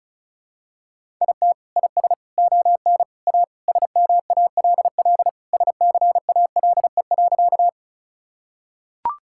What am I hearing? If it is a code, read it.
Morse code, heard as a keyed tone, about 36 wpm, 700 Hz. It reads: IT IS ON A SMALL SCALE.
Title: Sentences from Top 1000 Words - Review Part 5 - 35wpm